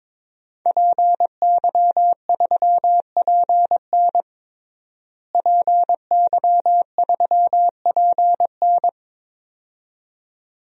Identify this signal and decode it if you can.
Morse code, heard as a keyed tone, about 22 wpm, 700 Hz. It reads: PY3PN PY3PN
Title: DXCC-100 Call Signs - 5 Characters - Repeated 2x - 22wpm